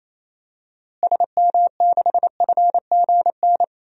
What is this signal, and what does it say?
Morse code, heard as a keyed tone, about 28 wpm, 700 Hz. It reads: SM6FGN
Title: DXCC-100 Call Signs - 6 Characters - Repeated 3x - 28wpm